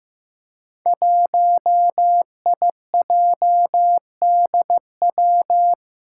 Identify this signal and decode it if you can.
Morse code, heard as a keyed tone, about 15 wpm, 700 Hz. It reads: 1IJDW
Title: Five Challenging Letter-Number Combinations 15wpm